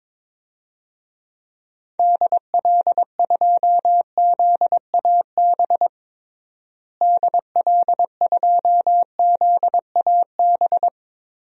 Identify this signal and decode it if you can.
Morse code, heard as a keyed tone, about 22 wpm, 700 Hz. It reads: DL2ZAB DL2ZAB
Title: DXCC-100 Call Signs - 6 Characters - Repeated 3x - 22wpm